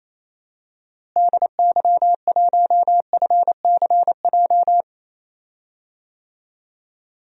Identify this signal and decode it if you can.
Morse code, heard as a keyed tone, about 28 wpm, 700 Hz. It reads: DY1FCJ
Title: DXCC-100 Call Signs - 6 Characters - Repeated 3x - 28wpm